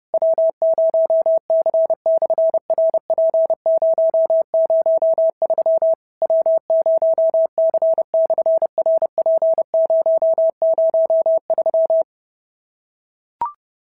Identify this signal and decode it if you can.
Morse code, heard as a keyed tone, about 30 wpm, 650 Hz. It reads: W0C/RP003 W0C/RP003
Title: SOTA QSO Practice 30wpm